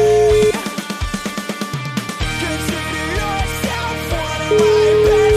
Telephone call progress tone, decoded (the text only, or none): ringback tone